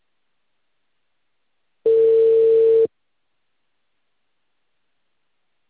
Ringback tone (China, 450 Hz).